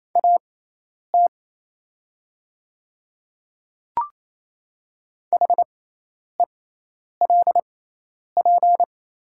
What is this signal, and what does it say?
Morse code, heard as a keyed tone, about 28 wpm, spaced as for 10 wpm, 700 Hz. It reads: AT HELP
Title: 6x Character Spacing - Top 200 Words - 28wpm